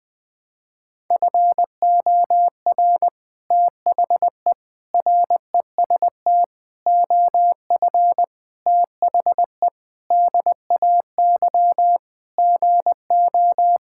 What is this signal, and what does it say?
Morse code, heard as a keyed tone, about 20 wpm, 700 Hz. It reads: FOR THE REST OF THE DAY GO
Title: Sentences with 3-Word Phrases - Spoken Beginning - 20wpm